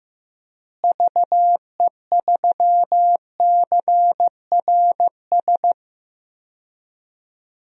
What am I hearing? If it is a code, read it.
Morse code, heard as a keyed tone, about 16 wpm, 700 Hz. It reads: VE3CRS
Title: DXCC-100 Call Signs - 6 Characters - Repeated 2x - 15wpm